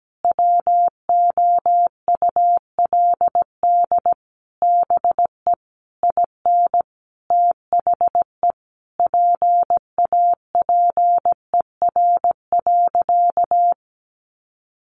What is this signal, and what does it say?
Morse code, heard as a keyed tone, about 17 wpm, 700 Hz. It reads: WOULD BE IN THE PAPER.
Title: Sentences from Top 300 Words 17wpm